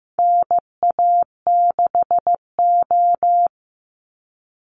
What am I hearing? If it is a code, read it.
Morse code, heard as a keyed tone, about 15 wpm, 700 Hz. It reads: NA6O
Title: NA SOTA Chaser Callsigns 15wpm